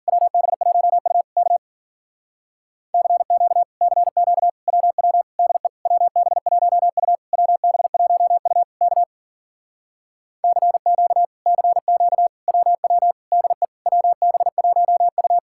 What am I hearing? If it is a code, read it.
Morse code, heard as a keyed tone, about 55 wpm, 700 Hz. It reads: WB1U K CQ CQ WW DE WB1U WB1U K CQ CQ WW DE WB1U